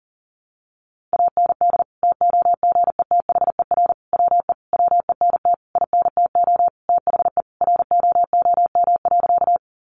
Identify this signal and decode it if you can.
Morse code, heard as a keyed tone, about 40 wpm, 700 Hz. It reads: AND TOGETHER WE WENT INTO THE ROOM.
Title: Sentences from Top 600 Words 40wpm